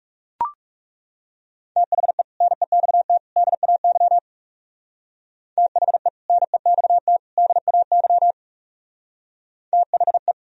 Morse code, heard as a keyed tone, about 36 wpm, 700 Hz. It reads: THE NEXT DAY THE NEXT DAY THE
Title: Speed-Racing - Sets of 3 Words 30wpm